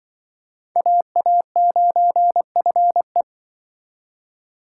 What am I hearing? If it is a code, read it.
Morse code, heard as a keyed tone, about 24 wpm, 700 Hz. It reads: AA9FE
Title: Speed Racing - USA 2x2 Call Signs 20wpm